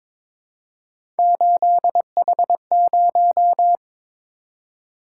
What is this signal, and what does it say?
Morse code, heard as a keyed tone, about 22 wpm, 700 Hz. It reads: 8H0
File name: practice / 22wpm / Three Letter-Number Combinations - 22wpm